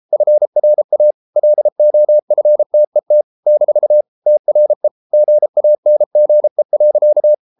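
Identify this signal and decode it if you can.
Morse code, heard as a keyed tone, about 33 wpm, 600 Hz. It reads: FRA LOFTET - TRE GANGE.
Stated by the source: Morse code at 33 wpm, QRQ.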